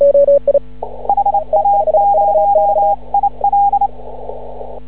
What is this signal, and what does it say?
Morse code, heard as a keyed tone, about 28 wpm, 810 Hz. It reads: HA0IL
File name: CW / ha6oi 2